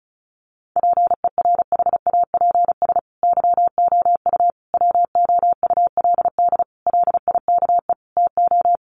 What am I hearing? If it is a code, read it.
Morse code, heard as a keyed tone, about 34 wpm, 700 Hz. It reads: PERHAPS YOU WOULD LIKE TO
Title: Sentences from Top 500 Words 35wpm